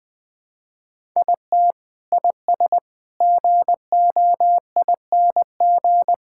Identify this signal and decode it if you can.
Morse code, heard as a keyed tone, about 20 wpm, 700 Hz. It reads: IT IS GOING
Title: Build A Phrase - 5-Word Phrases - 20wpm